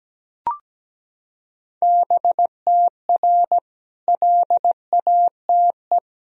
Morse code, heard as a keyed tone, about 17 wpm, 700 Hz. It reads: BTR LATE
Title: Sentences from Top 400 Words and Abbreviations 17wpm